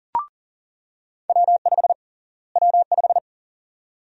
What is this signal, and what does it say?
Morse code, heard as a keyed tone, about 40 wpm, 700 Hz. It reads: W5 W5
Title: USA Call Sign Prefixes - Repeated 2x - 40wpm